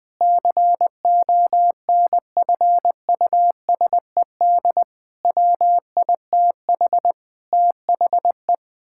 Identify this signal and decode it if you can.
Morse code, heard as a keyed tone, about 20 wpm, 700 Hz. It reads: CONFUSED WITH THE
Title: Sets of 6 Words - 20wpm